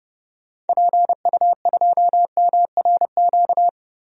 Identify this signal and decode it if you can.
Morse code, heard as a keyed tone, about 32 wpm, 700 Hz. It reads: PU2MRQ
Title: DXCC-100 Call Signs - 6 Characters - 30wpm